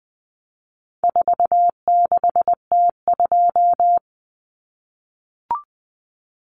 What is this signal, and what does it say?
Morse code, heard as a keyed tone, about 19 wpm, 700 Hz. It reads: 46T2
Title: Common Cut Numbers - 4-Digit 20wpm